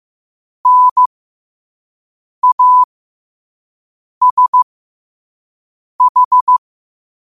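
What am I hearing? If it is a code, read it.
Morse code, heard as a keyed tone, about 15 wpm, spaced as for 5 wpm, 1000 Hz. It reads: NASH
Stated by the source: Morse code sent at 5 wpm.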